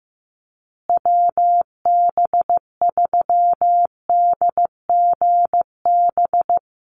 Morse code, heard as a keyed tone, about 15 wpm, 700 Hz. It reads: WB3DGB
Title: USA Call Signs Repeated 3x 15wpm